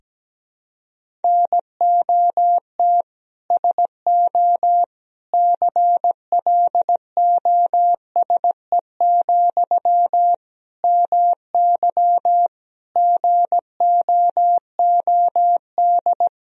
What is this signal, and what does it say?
Morse code, heard as a keyed tone, about 17 wpm, 700 Hz. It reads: NOT SO CLOSE, MY GOOD
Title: Sentences from Top 1000 Words - Review Part 1 - 17wpm